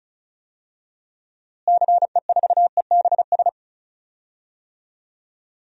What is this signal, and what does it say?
Morse code, heard as a keyed tone, about 34 wpm, 700 Hz. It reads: CE4EBS
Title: DXCC-100 Call Signs - 6 Characters - Repeated 2x - 35wpm